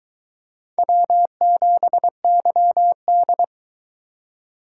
Morse code, heard as a keyed tone, about 21 wpm, 700 Hz. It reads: W7YD